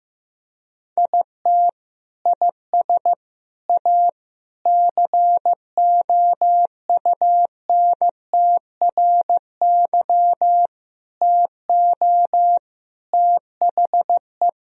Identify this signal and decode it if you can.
Morse code, heard as a keyed tone, about 15 wpm, 700 Hz. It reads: IT IS A COUNTRY TO THE
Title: Sentences from Top 1000 Words - Review Part 1 - 15wpm